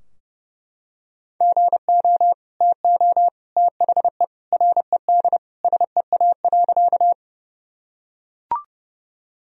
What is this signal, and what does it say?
Morse code, heard as a keyed tone, about 32 wpm, 700 Hz. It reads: GO TO THE RED SEA.
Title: Sentences from Top 1000 Words - Review Part 2 - 30wpm